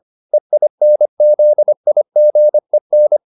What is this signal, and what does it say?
Morse code, heard as a keyed tone, about 25 wpm, 600 Hz. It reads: EINZIGEN